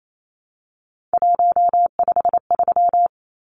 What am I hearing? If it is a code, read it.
Morse code, heard as a keyed tone, about 28 wpm, 700 Hz. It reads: 153
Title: Numbers - 3-Digit 28wpm